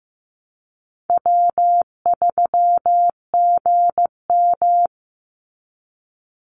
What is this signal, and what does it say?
Morse code, heard as a keyed tone, about 15 wpm, 700 Hz. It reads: W3GM
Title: USA Call Signs 15wpm